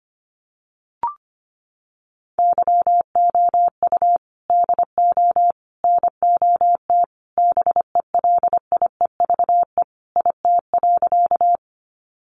Morse code, heard as a keyed tone, about 24 wpm, 700 Hz. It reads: YOU DO NOT BELIEVE IT.
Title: Sentences from Top 1000 Words 25wpm